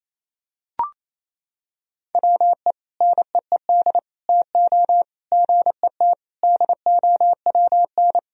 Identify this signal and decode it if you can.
Morse code, heard as a keyed tone, about 28 wpm, 700 Hz. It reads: WE NEED TO GET DOWN
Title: Sentences from Top 1000 Words - Review Part 1 - 28wpm